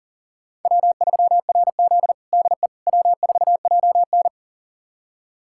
Morse code, heard as a keyed tone, about 40 wpm, 700 Hz. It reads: W3RZ DE W4JN
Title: USA Callsign DE Callsign - 4 Characters - 40wpm